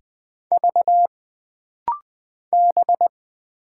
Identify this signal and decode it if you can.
Morse code, heard as a keyed tone, about 20 wpm, 700 Hz. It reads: V B
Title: Single Letters - Rapid-Fire 20wpm